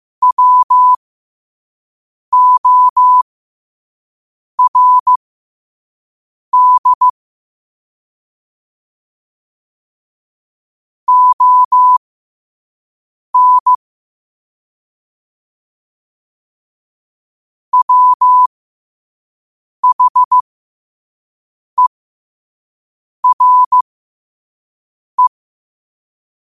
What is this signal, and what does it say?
Morse code, heard as a keyed tone, about 15 wpm, spaced as for 5 wpm, 1000 Hz. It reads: WORD ON WHERE
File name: qotdfiles / 5wpm